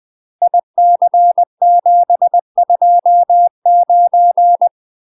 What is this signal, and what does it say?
Morse code, heard as a keyed tone, about 20 wpm, 700 Hz. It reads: IC729
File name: MASTER QSOs TRAINING/20WPM